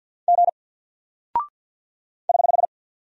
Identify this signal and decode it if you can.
Morse code, heard as a keyed tone, about 48 wpm, 700 Hz. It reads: G <HH>
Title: Single Character (Letter:Number:Punctuation:Prosign) Alt 50wpm